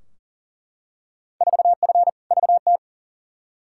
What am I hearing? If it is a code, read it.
Morse code, heard as a keyed tone, about 40 wpm, 700 Hz. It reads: 4F VT
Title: Field Day Exchanges Encore - 40wpm